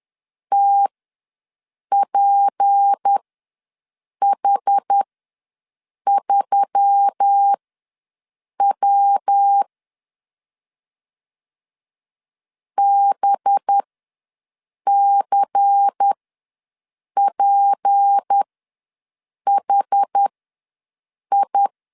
Morse code, heard as a keyed tone, about 10 wpm, spaced as for 6 wpm, 800 Hz. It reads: TPH3W BCPHI
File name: LEZIONI / 6 PKX0 / 6 P ws 10wpm 3751